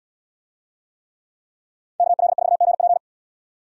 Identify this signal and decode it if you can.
Morse code, heard as a keyed tone, about 75 wpm, 700 Hz. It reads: KL4GP